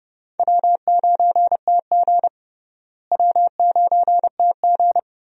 Morse code, heard as a keyed tone, about 30 wpm, 700 Hz. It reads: W9TG W9TG
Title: USA Call Signs - 4 Characters - Repeated 2x - 30wpm